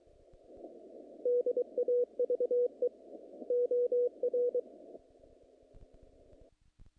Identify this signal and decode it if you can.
Morse code, heard as a keyed tone, about 26 wpm, 470 Hz. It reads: DAVE OR
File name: contests / naqp-cw-k5tr-op-k5pi- -2200